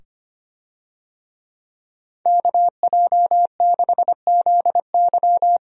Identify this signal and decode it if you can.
Morse code, heard as a keyed tone, about 25 wpm, 700 Hz. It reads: KJ6ZY
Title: USA Call Signs - 5 Characters - 25wpm